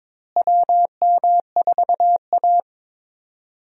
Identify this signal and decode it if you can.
Morse code, heard as a keyed tone, about 22 wpm, 700 Hz. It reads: WM4A
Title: USA Call Signs - 4 Characters - Repeated 3x - 22wpm